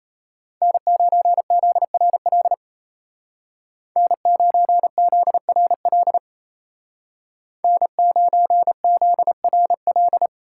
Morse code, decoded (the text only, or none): N9ZRL N9ZRL N9ZRL